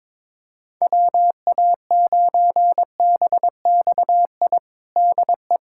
Morse code, heard as a keyed tone, about 22 wpm, 700 Hz. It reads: WA9BXI DE
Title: Practice QSO - Exchange 4 - Ending - 22wpm